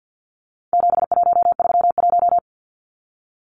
Morse code, heard as a keyed tone, about 50 wpm, 700 Hz. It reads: 7131
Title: Numbers - 4-Digit 50wpm